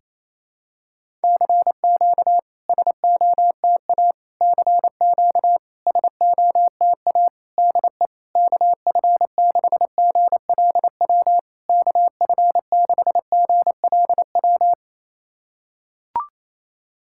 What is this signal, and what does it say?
Morse code, heard as a keyed tone, about 27 wpm, 690 Hz. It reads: CQ SOTA CQ SOTA DE KF6GLW KF6GLW
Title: SOTA QSO Practice 28wpm